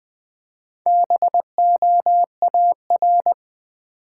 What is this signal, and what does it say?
Morse code, heard as a keyed tone, about 20 wpm, 700 Hz. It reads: BOAR
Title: Build A Word - 1 Letter At A Time - 5-Letter Words - 20wpm